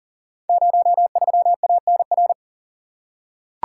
Morse code, heard as a keyed tone, about 42 wpm, 700 Hz. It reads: 03ANR
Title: Code Groups - 5 Letter:Numbers - 40wpm